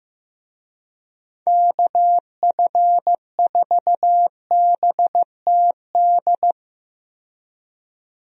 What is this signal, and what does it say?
Morse code, heard as a keyed tone, about 15 wpm, 700 Hz. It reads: KF4BTD
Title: USA Call Signs - 6 Characters - 15wpm